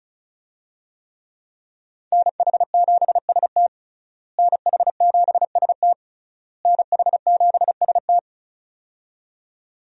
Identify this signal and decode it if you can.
Morse code, heard as a keyed tone, about 36 wpm, 700 Hz. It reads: NH7ST NH7ST NH7ST
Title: USA Call Signs - 5 Characters - Repeated 3x - 35wpm